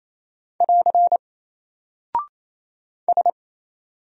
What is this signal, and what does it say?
Morse code, heard as a keyed tone, about 27 wpm, 700 Hz. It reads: <AR> S
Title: Single Character (Letter:Number:Punctuation:Prosign) Alt 28wpm